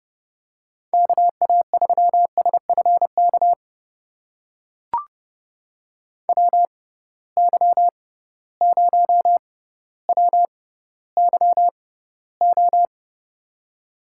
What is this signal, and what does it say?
Morse code, heard as a keyed tone, about 30 wpm, spaced as for 15 wpm, 700 Hz. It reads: KA3SFK WY0WYO
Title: USA Call Signs - 6 Characters - 6x Character Spacing - 30wpm